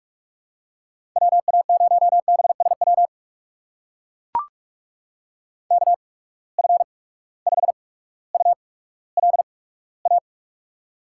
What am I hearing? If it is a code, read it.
Morse code, heard as a keyed tone, about 43 wpm, spaced as for 19 wpm, 700 Hz. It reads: WA0BSW KF5ULA
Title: USA Call Signs - 6 Characters - 8x Character Spacing - 45wpm